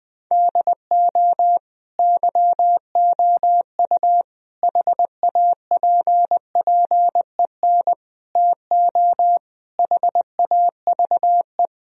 Morse code, heard as a keyed tone, about 20 wpm, 700 Hz. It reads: DO YOU HAPPEN TO HAVE
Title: Sentences from Top 1000 Words - Review Part 2 - 20wpm